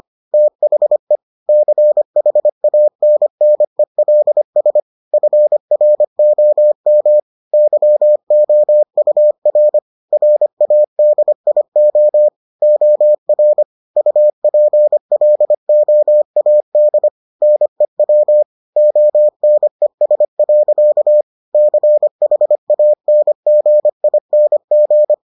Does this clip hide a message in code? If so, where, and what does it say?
Yes: Morse code from 0.3 to 25.1 s, heard as a keyed tone, about 25 wpm, 600 Hz. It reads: THE CHANNELS FROM YOUR RADIO OR UPLOAD NEW ONES. CHANGING